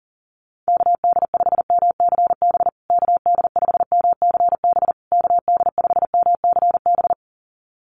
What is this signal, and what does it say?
Morse code, heard as a keyed tone, about 40 wpm, 700 Hz. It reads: KD5MCB KD5MCB KD5MCB
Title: USA Call Signs Repeated 3x 40wpm